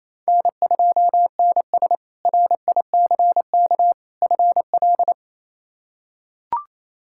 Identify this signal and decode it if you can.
Morse code, heard as a keyed tone, about 28 wpm, 700 Hz. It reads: N2NS RICK FL
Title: K1USN SST QSO Practice - 28wpm